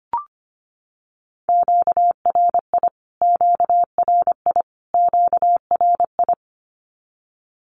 Morse code, heard as a keyed tone, about 25 wpm, 700 Hz. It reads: QRI QRI QRI
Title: Q-Codes Repeated 3x 25wpm